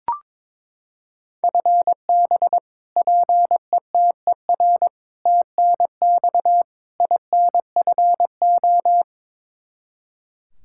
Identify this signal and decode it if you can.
Morse code, heard as a keyed tone, about 22 wpm, 700 Hz. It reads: FB PETER TNX INFO
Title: Practice QSO - Exchange 3 - More Info - 22wpm